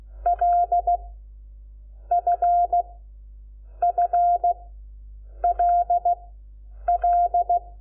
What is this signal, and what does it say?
Morse code, heard as a keyed tone, about 16 wpm, 700 Hz. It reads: L F F L L
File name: Lecciones / CLASE Nº5